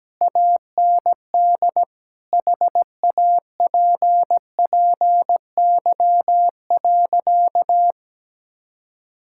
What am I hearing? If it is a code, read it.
Morse code, heard as a keyed tone, about 17 wpm, 700 Hz. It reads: AND HAPPY.
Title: Sentences from Top 1000 Words - Review Part 5 - 17wpm